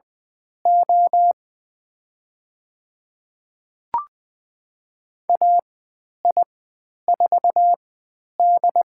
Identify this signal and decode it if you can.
Morse code, heard as a keyed tone, about 20 wpm, spaced as for 10 wpm, 700 Hz. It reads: O AI4D
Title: 102 Lesson 17g - Letter u - Call Signs 20wpm